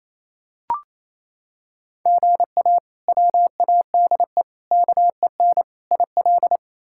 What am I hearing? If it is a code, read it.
Morse code, heard as a keyed tone, about 28 wpm, 700 Hz. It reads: GA WADE KEN IL
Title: K1USN SST QSO Practice - 28wpm